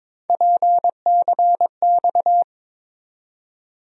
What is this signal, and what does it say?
Morse code, heard as a keyed tone, about 22 wpm, 700 Hz. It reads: PCX
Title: USA Call Sign Postfixes - 3 Letters 22wpm